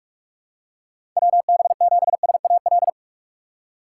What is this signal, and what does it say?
Morse code, heard as a keyed tone, about 46 wpm, 700 Hz. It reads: WB7SAL
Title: USA Call Signs - 6 Characters - 2x Character Spacing - 45wpm